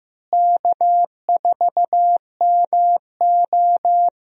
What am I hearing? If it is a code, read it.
Morse code, heard as a keyed tone, about 15 wpm, 700 Hz. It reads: K4MO
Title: USA Call Signs - 4 Characters - Repeated 3x - 15wpm